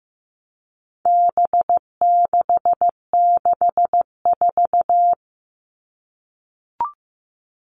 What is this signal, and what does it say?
Morse code, heard as a keyed tone, about 15 wpm, 700 Hz. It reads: B664
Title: All Cut Numbers - 4-Digit 15wpm